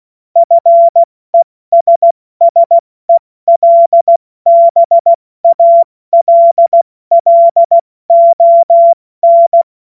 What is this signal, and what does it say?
Morse code, heard as a keyed tone, about 16 wpm, 680 Hz. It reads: FESSELBALLON
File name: Morsekurs / Witze / Witz 16. August 70 F80